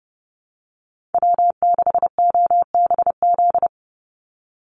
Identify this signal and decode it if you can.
Morse code, heard as a keyed tone, about 30 wpm, 700 Hz. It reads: W6OBZ
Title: USA Call Signs 30wpm